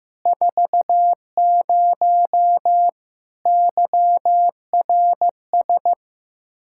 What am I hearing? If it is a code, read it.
Morse code, heard as a keyed tone, about 15 wpm, 700 Hz. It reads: 40 YRS
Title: Practice QSO - Exchange 3 - More Info - 15wpm